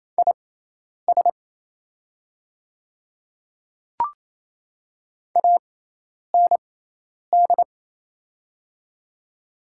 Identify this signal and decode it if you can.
Morse code, heard as a keyed tone, about 28 wpm, spaced as for 10 wpm, 690 Hz. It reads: IS AND
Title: 6x Character Spacing - Top 100 Words - 28wpm